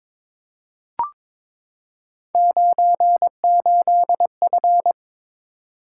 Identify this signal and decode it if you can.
Morse code, heard as a keyed tone, about 23 wpm, 700 Hz. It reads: 98F
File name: practice / 22wpm / Three Letter-Number Combinations - 22wpm